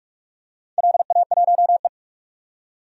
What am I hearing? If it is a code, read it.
Morse code, heard as a keyed tone, about 44 wpm, 700 Hz. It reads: LA1E
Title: DXCC-100 Call Signs - 4 Characters - 45wpm